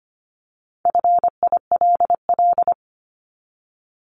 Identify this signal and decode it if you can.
Morse code, heard as a keyed tone, about 25 wpm, 700 Hz. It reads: FILL
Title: Top 500 Words Repeated 25wpm